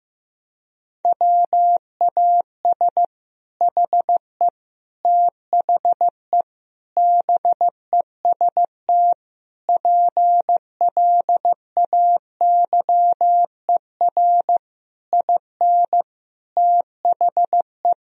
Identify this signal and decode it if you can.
Morse code, heard as a keyed tone, about 15 wpm, 700 Hz. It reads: WAS HE THE BEST PLAYER IN THE
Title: Sentences from Top 1000 Words - Review Part 4 - 15wpm